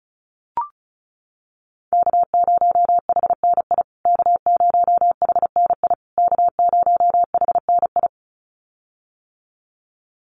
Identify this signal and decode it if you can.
Morse code, heard as a keyed tone, about 34 wpm, 700 Hz. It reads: K0HNI K0HNI K0HNI
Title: USA Call Signs Repeated 3x 35wpm